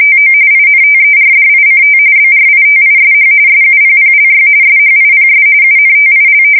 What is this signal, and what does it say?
RTTY, about 45.45 baud. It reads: TO INTERDICT BRITISH SUPPLY SHIPS UNTIL
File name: fsk170 / usnavy1